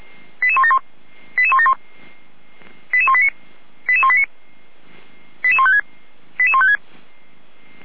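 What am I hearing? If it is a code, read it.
Five-tone selective call (ZVEI), heard as quick runs of five tones, tones of about 80 ms. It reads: 88171 88171 88189 88189 88156 88156
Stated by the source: ZVEI five-tone selective call 88505 received on SDR.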